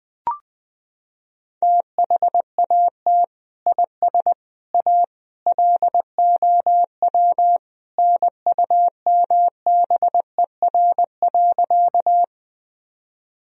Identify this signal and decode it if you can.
Morse code, heard as a keyed tone, about 20 wpm, 700 Hz. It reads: THAT IS A LOW NUMBER.
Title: Sentences from Top 1000 Words - Review Part 1 - 20wpm